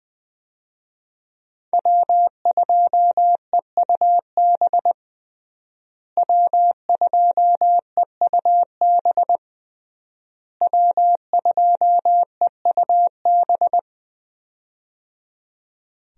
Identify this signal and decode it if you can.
Morse code, heard as a keyed tone, about 20 wpm, 700 Hz. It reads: W2EUB W2EUB W2EUB
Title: USA Call Signs - 5 Characters - Repeated 3x - 20wpm